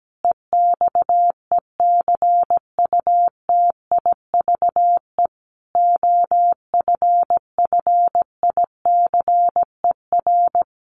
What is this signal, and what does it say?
Morse code, heard as a keyed tone, about 17 wpm, 700 Hz. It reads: EXECUTIVE OFFICER
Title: Sets of 5 Words Encore2 17wpm